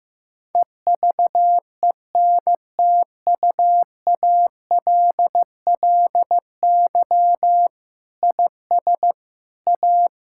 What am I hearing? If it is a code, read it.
Morse code, heard as a keyed tone, about 15 wpm, 700 Hz. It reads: EVENTUALLY IS A
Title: Sentences from Top 1000 Words - Review Part 6 - 15wpm